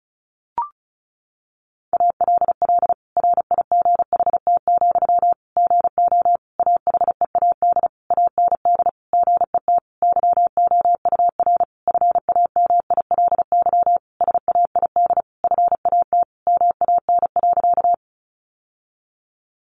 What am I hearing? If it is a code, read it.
Morse code, heard as a keyed tone, about 34 wpm, 700 Hz. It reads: ALL RIGHT, GO AHEAD AND GET YOUR FAMILY SAID FAT MAN.
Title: Sentences from Top 1100 Words 35wpm